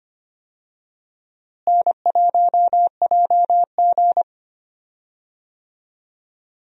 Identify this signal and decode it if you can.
Morse code, heard as a keyed tone, about 25 wpm, 700 Hz. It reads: N1JG
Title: USA Call Signs - 4 Characters - Repeated 3x - 25wpm